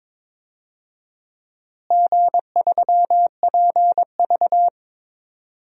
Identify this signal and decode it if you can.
Morse code, heard as a keyed tone, about 22 wpm, 700 Hz. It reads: G3PV
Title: DXCC-100 Call Signs - 4 Characters - Repeated 3x - 22wpm